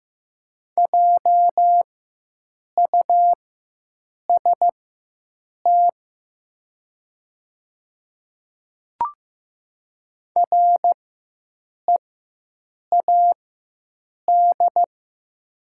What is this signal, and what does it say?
Morse code, heard as a keyed tone, about 15 wpm, spaced as for 7 wpm, 700 Hz. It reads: JUST READ
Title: 4x Character Spacing - Top 300 Words - 15wpm